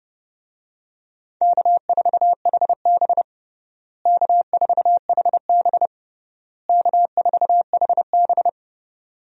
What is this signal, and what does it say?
Morse code, heard as a keyed tone, about 32 wpm, 700 Hz. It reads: K4HB K4HB K4HB
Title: USA Call Signs - 4 Characters - Repeated 3x - 30wpm